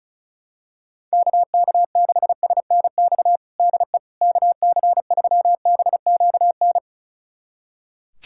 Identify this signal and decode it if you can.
Morse code, heard as a keyed tone, about 36 wpm, 700 Hz. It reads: KK6SNX DE KC3BQN
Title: USA Callsign DE Callsign - 6 Characters - 35wpm